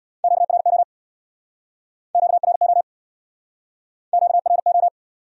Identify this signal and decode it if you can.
Morse code, heard as a keyed tone, about 64 wpm, 700 Hz. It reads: QRO QRO QRO